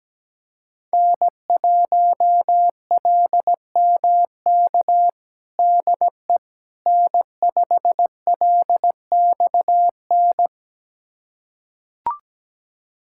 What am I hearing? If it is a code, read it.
Morse code, heard as a keyed tone, about 17 wpm, 700 Hz. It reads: N1LMK DE N5LXN
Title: USA Callsign DE Callsign - 5 Characters - 17wpm